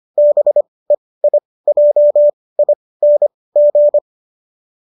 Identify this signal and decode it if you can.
Morse code, heard as a keyed tone, about 25 wpm, spaced as for 18 wpm, 590 Hz. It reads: BEIJING